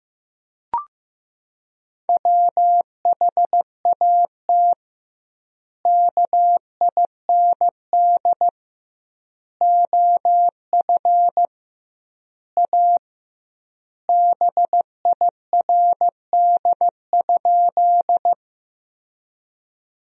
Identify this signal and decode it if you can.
Morse code, heard as a keyed tone, about 15 wpm, 700 Hz. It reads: WHAT KIND OF A BIRD?
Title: 2x Word-Spacing - Sentences from Top 400 Words - 15wpm